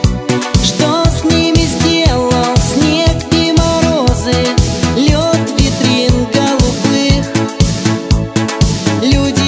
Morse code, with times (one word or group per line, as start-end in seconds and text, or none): none